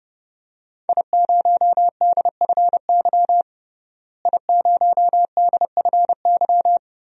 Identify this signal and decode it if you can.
Morse code, heard as a keyed tone, about 32 wpm, 700 Hz. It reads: I0DFY I0DFY
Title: DXCC-100 Call Signs - 5 Characters - Repeated 3x - 30wpm